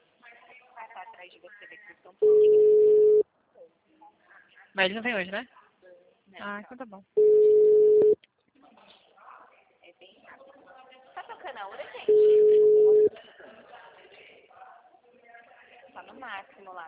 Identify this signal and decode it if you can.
Ringback tone (Europe, 425 Hz), with speech.